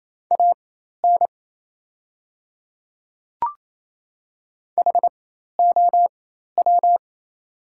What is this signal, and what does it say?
Morse code, heard as a keyed tone, about 28 wpm, spaced as for 13 wpm, 700 Hz. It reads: AN HOW
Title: 4x Character Spacing - Top 100 Words - 28wpm